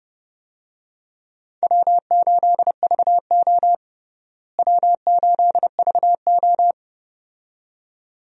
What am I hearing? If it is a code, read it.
Morse code, heard as a keyed tone, about 32 wpm, 700 Hz. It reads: W8VO W8VO
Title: USA Call Signs - 4 Characters - Repeated 2x - 30wpm